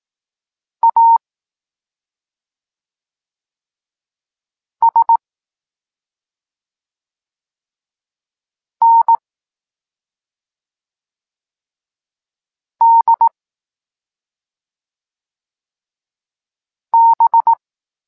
Morse code, heard as a keyed tone, about 18 wpm, 920 Hz. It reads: A S N D B
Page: Tekst 14 – doskonalenie odbioru poznanych liter